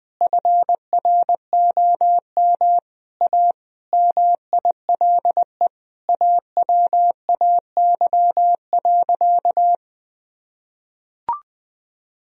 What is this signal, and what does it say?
Morse code, heard as a keyed tone, about 20 wpm, 700 Hz. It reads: FROM A MILE AWAY.
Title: Sentences from Top 1000 Words - Review Part 3 - 20wpm